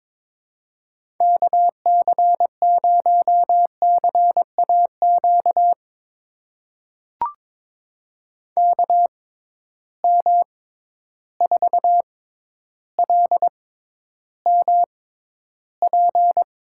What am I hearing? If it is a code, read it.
Morse code, heard as a keyed tone, about 22 wpm, spaced as for 11 wpm, 700 Hz. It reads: KC0CAQ KM4LMP
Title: USA Call Signs - 6 Characters - 6x Character Spacing - 22wpm